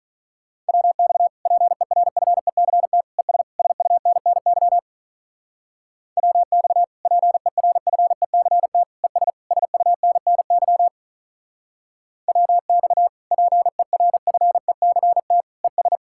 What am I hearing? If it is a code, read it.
Morse code, heard as a keyed tone, about 44 wpm, 700 Hz. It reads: WX PERFECT ES SUNNY WX PERFECT ES SUNNY WX PERFECT ES